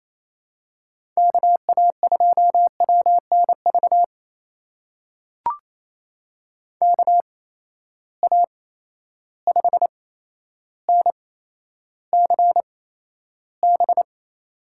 Morse code, heard as a keyed tone, about 28 wpm, spaced as for 12 wpm, 700 Hz. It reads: KA2WNV KA5NCB
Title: USA Call Signs - 6 Characters - 8x Character Spacing - 28wpm